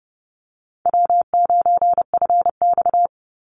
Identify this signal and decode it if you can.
Morse code, heard as a keyed tone, about 30 wpm, 700 Hz. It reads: W9FX
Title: Easy Call Signs Repeated 3x - 30wpm